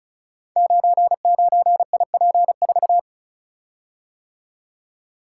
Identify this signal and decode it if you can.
Morse code, heard as a keyed tone, about 36 wpm, 700 Hz. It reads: 99IP4
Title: Code Groups - 5 Letter:Numbers - 35wpm